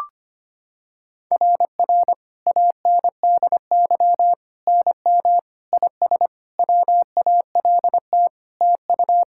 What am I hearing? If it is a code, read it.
Morse code, heard as a keyed tone, about 25 wpm, 700 Hz. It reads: RR ANDY NM IS WALT TU
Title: SOTA QSO Practice 25wpm